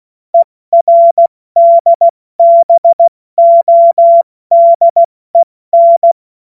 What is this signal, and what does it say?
Morse code, heard as a keyed tone, about 16 wpm, spaced as for 14 wpm, 680 Hz. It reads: ERDBODEN
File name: Morsekurs / Witze / Witz 31.Oktober 2023 70 F80